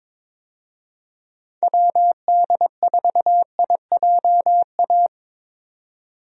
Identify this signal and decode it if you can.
Morse code, heard as a keyed tone, about 22 wpm, 700 Hz. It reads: WD4IJA
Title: USA Call Signs - 6 Characters - 22wpm